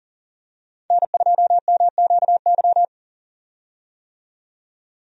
Morse code, heard as a keyed tone, about 40 wpm, 700 Hz. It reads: N2MQY
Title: USA Call Signs - 5 Characters - 40wpm